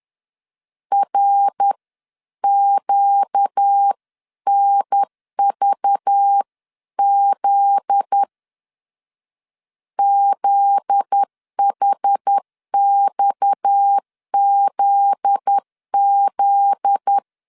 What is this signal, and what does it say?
Morse code, heard as a keyed tone, about 10 wpm, 800 Hz. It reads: RQNVZ ZHXZZ
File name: LEZIONI / 7 GF9Z / 7 GF9Z 10wpm 2611